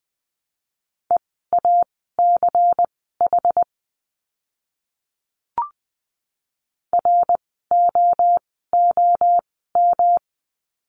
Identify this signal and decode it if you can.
Morse code, heard as a keyed tone, about 20 wpm, spaced as for 15 wpm, 700 Hz. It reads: EACH ROOM
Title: 115 Lesson 19f - Letter w - Words 20wpm